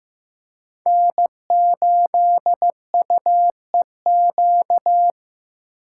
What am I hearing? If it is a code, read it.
Morse code, heard as a keyed tone, about 18 wpm, 700 Hz. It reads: N8UEQ
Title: Speed Racing - USA 1x3 Call Signs 15wpm